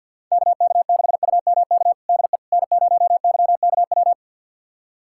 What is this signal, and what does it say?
Morse code, heard as a keyed tone, about 51 wpm, 700 Hz. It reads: KK6UMK DE N0YKW
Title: USA Callsign DE Callsign 50wpm